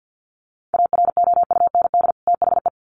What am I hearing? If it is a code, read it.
Morse code, heard as a keyed tone, about 50 wpm, 700 Hz. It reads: AROUND THE
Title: Sets of 2 Words 50wpm